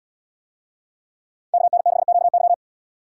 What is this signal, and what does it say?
Morse code, heard as a keyed tone, about 77 wpm, 700 Hz. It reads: KN6QY